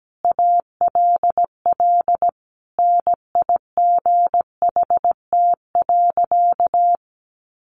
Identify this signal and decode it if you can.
Morse code, heard as a keyed tone, about 17 wpm, 700 Hz. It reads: ALL NIGHT.
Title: Sentences from Top 300 Words 17wpm